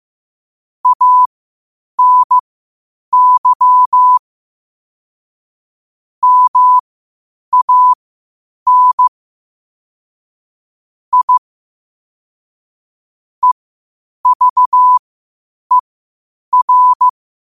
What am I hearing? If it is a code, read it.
Morse code, heard as a keyed tone, about 15 wpm, spaced as for 8 wpm, 1000 Hz. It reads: ANY MAN I EVER
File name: qotdfiles / 8wpm